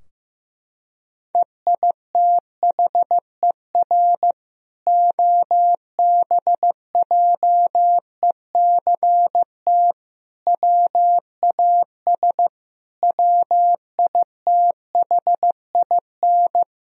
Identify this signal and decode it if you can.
Morse code, heard as a keyed tone, about 15 wpm, 700 Hz. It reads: EITHER OBJECT WAS WITHIN
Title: Sentences from Top 1000 Words - Review Part 4 - 15wpm